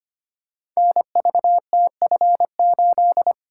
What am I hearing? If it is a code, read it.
Morse code, heard as a keyed tone, about 25 wpm, 700 Hz. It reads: NVTF8
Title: Code Groups - 5 Letter:Numbers - 25wpm